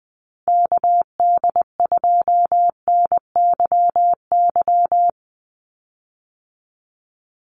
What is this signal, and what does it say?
Morse code, heard as a keyed tone, about 20 wpm, 700 Hz. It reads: KD2NYY
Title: USA Call Signs 20wpm